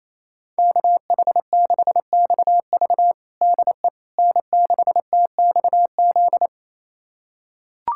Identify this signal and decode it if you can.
Morse code, heard as a keyed tone, about 28 wpm, 700 Hz. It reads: KH6XV DE N6TXZ
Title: USA Callsign DE Callsign - 5 Characters - 28wpm